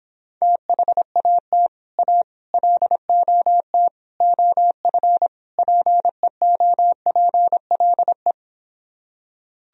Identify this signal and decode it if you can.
Morse code, heard as a keyed tone, about 26 wpm, 700 Hz. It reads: THAT A LOT OF PEOPLE